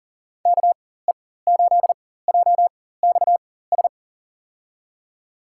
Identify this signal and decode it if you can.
Morse code, heard as a keyed tone, about 37 wpm, spaced as for 18 wpm, 700 Hz. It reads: KE8JXS
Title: USA Call Signs - 6 Characters - 4x Character Spacing - 40wpm